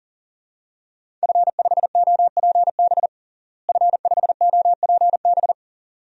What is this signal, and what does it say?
Morse code, heard as a keyed tone, about 40 wpm, 700 Hz. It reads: F5OPB F5OPB
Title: DXCC-100 Call Signs - 5 Characters - Repeated 2x - 40wpm